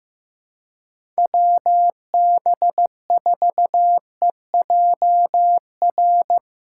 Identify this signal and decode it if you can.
Morse code, heard as a keyed tone, about 15 wpm, 700 Hz. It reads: WB4EJR
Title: USA Callsign DE Callsign 15wpm